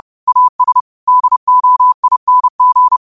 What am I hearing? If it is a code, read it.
Morse code, heard as a keyed tone, about 30 wpm, 1000 Hz. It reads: AS DOING
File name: qotdfiles / 30wpm